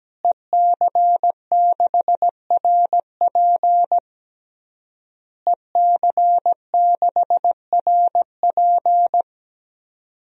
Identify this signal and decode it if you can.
Morse code, heard as a keyed tone, about 17 wpm, 700 Hz. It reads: EC6RP EC6RP
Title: DXCC-100 Call Signs - 5 Characters - Repeated 3x - 17wpm